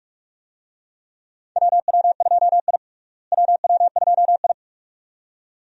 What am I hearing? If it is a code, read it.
Morse code, heard as a keyed tone, about 42 wpm, 700 Hz. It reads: WW2I WW2I
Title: USA Call Signs - 4 Characters - Repeated 2x - 45wpm